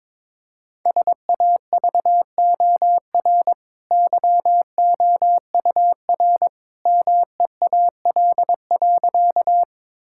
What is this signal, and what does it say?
Morse code, heard as a keyed tone, about 22 wpm, 690 Hz. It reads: SAVOR YOUR MEAL.
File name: practice / 22wpm / Easy Sentences with Hints - 22wpm